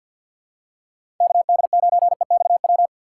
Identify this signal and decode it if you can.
Morse code, heard as a keyed tone, about 50 wpm, 700 Hz. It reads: KD9EXW
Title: USA Call Signs - 6x Character Spacing - 50wpm